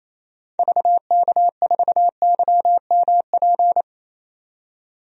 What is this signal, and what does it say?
Morse code, heard as a keyed tone, about 27 wpm, 700 Hz. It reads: VK4YMP
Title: DXCC-100 Call Signs - 6 Characters - 28wpm